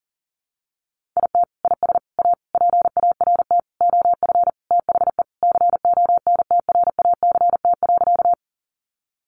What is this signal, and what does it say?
Morse code, heard as a keyed tone, about 37 wpm, 700 Hz. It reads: IT IS A PART OF THE CONTRACT.
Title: Sentences from Top 1400 Words 40wpm